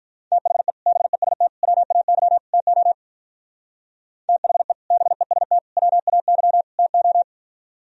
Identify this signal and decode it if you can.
Morse code, heard as a keyed tone, about 44 wpm, 700 Hz. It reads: THE BEST WAY TO THE BEST WAY TO